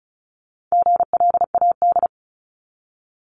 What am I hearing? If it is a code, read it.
Morse code, heard as a keyed tone, about 36 wpm, 700 Hz. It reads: GLAD
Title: Multi-letter Phonemes 35wpm